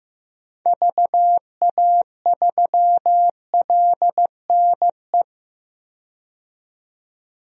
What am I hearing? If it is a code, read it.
Morse code, heard as a keyed tone, about 15 wpm, 700 Hz. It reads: VA3LNE
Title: DXCC-100 Call Signs - 6 Characters - 15wpm